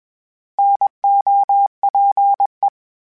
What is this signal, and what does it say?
Morse code, heard as a keyed tone, about 20 wpm, 800 Hz. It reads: NOPE
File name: Misc / Misc-mp3s / Misc @20